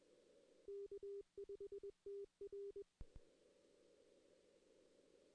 Morse code, heard as a keyed tone, about 21 wpm, 400 Hz. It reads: K5TR
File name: 2016 arrl dx cw / 2016-ARRL-DX-CW-K5TR- - 0-3